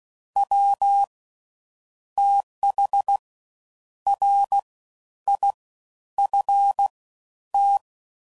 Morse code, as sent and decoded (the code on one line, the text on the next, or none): .-- / - .... / .-. / .. / ..-. / -
W TH R I F T